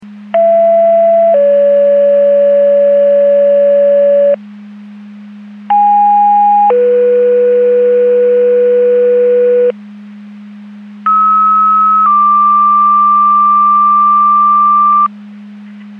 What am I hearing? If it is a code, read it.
Two-tone page, heard as a one-second tone, then a three-second tone, about 670 then 570 Hz, about 820 then 490 Hz, about 1290 then 1180 Hz. It is heard from 0.3 to 15.1 s.
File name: Portage 154.385 Paging